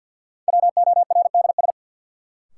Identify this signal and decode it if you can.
Morse code, heard as a keyed tone, about 49 wpm, 700 Hz. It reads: WORDS
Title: Sets of 3 Words - New Word Drill - 50wpm